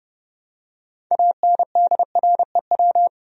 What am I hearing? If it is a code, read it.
Morse code, heard as a keyed tone, about 30 wpm, 700 Hz. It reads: ANDREW
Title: USA Common Names - 30wpm